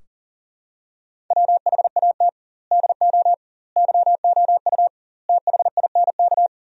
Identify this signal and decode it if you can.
Morse code, heard as a keyed tone, about 40 wpm, 700 Hz. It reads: WHAT DO YOU THINK
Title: 2x Word-Spacing - Sentences from Top 200 Words - 40wpm